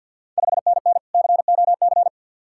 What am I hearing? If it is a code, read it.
Morse code, heard as a keyed tone, about 52 wpm, 690 Hz. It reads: 5NN COC
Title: Ontario QSO Party Exchange 50wpm